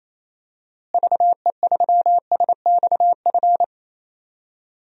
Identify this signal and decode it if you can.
Morse code, heard as a keyed tone, about 27 wpm, 700 Hz. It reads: VE3SXF
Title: DXCC-100 Call Signs - 6 Characters - Repeated 2x - 28wpm